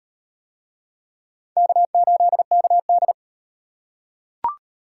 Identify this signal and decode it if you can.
Morse code, heard as a keyed tone, about 37 wpm, 700 Hz. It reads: K8KD